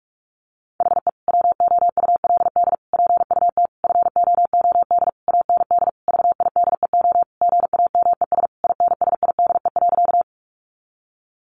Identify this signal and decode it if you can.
Morse code, heard as a keyed tone, about 46 wpm, 700 Hz. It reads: HE WOULD PUT FOOD AND VIDEO GAMES INSIDE.
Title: Easy Phrases 45wpm